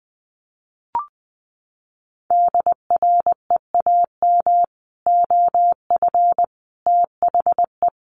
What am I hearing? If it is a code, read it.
Morse code, heard as a keyed tone, about 20 wpm, 700 Hz. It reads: DREAM OF THE
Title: Sentences from Top 1500 Words 20wpm